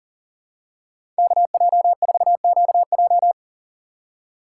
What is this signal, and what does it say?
Morse code, heard as a keyed tone, about 40 wpm, 700 Hz. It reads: KJ4QJ
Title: USA Call Signs - 5 Characters - 4x Character Spacing - 40wpm